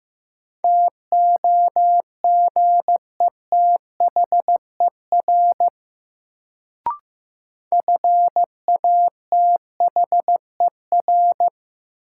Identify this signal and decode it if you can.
Morse code, heard as a keyed tone, about 15 wpm, 700 Hz. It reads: TOGETHER FATHER
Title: Sentences from Top 300 Words - New Word Drill - 15wpm